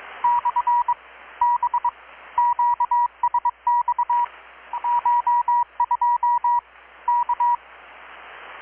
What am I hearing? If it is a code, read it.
Morse code, heard as a keyed tone, about 21 wpm, 960 Hz, with radio noise. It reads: / B QSX CH2 K
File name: blog / swl / 6-abril-2019-12695-USB-KPS